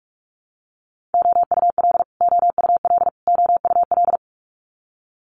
Morse code, heard as a keyed tone, about 46 wpm, 700 Hz. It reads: OUL OUL OUL
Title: Multi-letter Phonemes Repeated 3x 45wpm